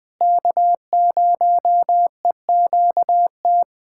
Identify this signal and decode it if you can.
Morse code, heard as a keyed tone, about 21 wpm, 700 Hz. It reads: K0EQT
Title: USA Callsign DE Callsign - 5 Characters - 20wpm